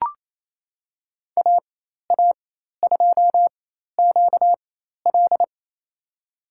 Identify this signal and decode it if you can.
Morse code, heard as a keyed tone, about 28 wpm, spaced as for 13 wpm, 700 Hz. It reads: AA2QL
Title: USA Call Signs - 5 Characters - 4x Character Spacing - 28wpm